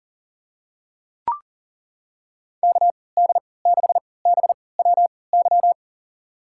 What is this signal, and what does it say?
Morse code, heard as a keyed tone, about 40 wpm, spaced as for 23 wpm, 700 Hz. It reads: KD6BWY
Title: USA Call Signs - 3x Character Spacing - 40wpm